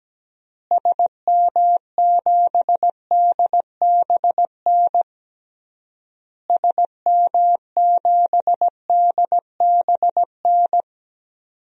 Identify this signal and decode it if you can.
Morse code, heard as a keyed tone, about 17 wpm, 700 Hz. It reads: SM7DBN SM7DBN
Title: DXCC-100 Call Signs - 6 Characters - Repeated 3x - 17wpm